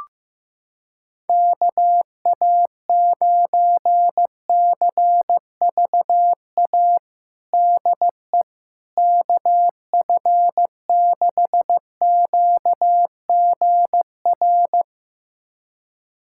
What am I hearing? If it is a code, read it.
Morse code, heard as a keyed tone, about 15 wpm, 700 Hz. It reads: KA9CVA DE KF6QGR
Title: USA Callsign DE Callsign 15wpm